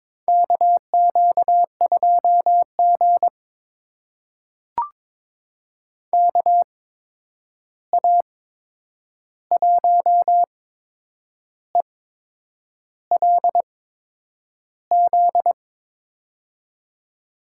Morse code, heard as a keyed tone, about 22 wpm, spaced as for 8 wpm, 700 Hz. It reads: KQ2G KA1ELZ
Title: USA Call Signs - 8x Character Spacing - 22wpm